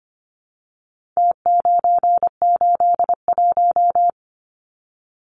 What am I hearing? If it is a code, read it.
Morse code, heard as a keyed tone, about 25 wpm, 700 Hz. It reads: T981
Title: Numbers - Contest Style - 4-Digit 25wpm